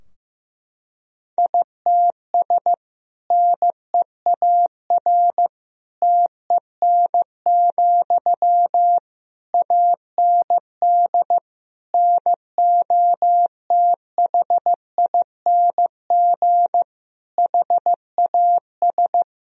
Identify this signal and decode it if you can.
Morse code, heard as a keyed tone, about 15 wpm, 700 Hz. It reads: ITS NEAR TEN, AND NOTHING HAS
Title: Sentences from Top 1000 Words - Review Part 2 - 15wpm